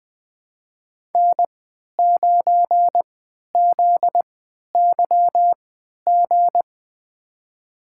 Morse code, heard as a keyed tone, about 20 wpm, spaced as for 11 wpm, 700 Hz. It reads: N9ZYG
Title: USA Call Signs - 5 Characters - 3x Character Spacing - 20wpm